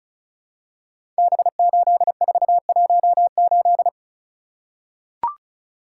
Morse code, heard as a keyed tone, about 36 wpm, 700 Hz. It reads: B8418
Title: Code Groups - 5 Letter:Numbers - 35wpm